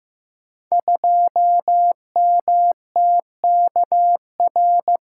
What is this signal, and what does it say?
Morse code, heard as a keyed tone, about 15 wpm, 700 Hz. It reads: 2MTKR
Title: Code Groups - 5 Letter:Numbers - 15wpm